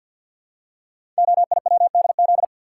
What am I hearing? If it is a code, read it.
Morse code, heard as a keyed tone, about 50 wpm, 700 Hz. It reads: OIWDZ
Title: Code Groups - 5 Letters - 50wpm